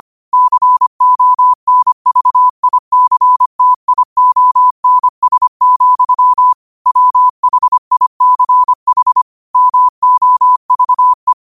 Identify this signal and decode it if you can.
Morse code, heard as a keyed tone, about 25 wpm, 1000 Hz. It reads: CONVICTIONS, WHICH MOVE
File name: qotdfiles / 25wpm